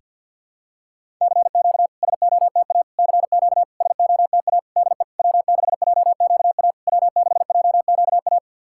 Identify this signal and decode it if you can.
Morse code, heard as a keyed tone, about 50 wpm, 690 Hz. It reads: CQ SOTA CQ SOTA DE W6JQA W6JQA
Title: SOTA QSO Practice 50wpm